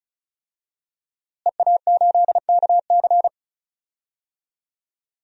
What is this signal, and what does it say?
Morse code, heard as a keyed tone, about 34 wpm, 700 Hz. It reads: EA8KC
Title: DXCC-100 Call Signs - 5 Characters - Repeated 2x - 35wpm